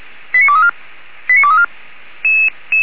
Five-tone selective call (ZVEI), heard as quick runs of five tones, tones of about 70 ms. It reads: 89234 89234